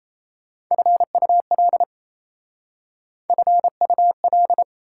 Morse code, heard as a keyed tone, about 35 wpm, 690 Hz. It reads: FUL FUL
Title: Speed-Racing Prefixes and Suffixes 28wpm